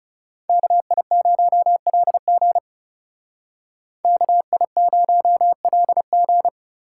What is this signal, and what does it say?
Morse code, heard as a keyed tone, about 39 wpm, 700 Hz. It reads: KI0LG KI0LG